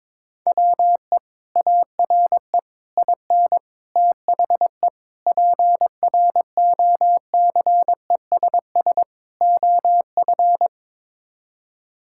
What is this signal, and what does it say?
Morse code, heard as a keyed tone, about 22 wpm, 700 Hz. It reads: WE ARE IN THE PROCESS OF
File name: practice / 22wpm / Sets of 6 Words - 22wpm